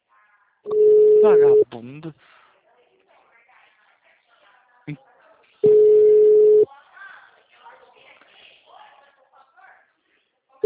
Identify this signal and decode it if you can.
Ringback tone (Europe, 425 Hz), with speech.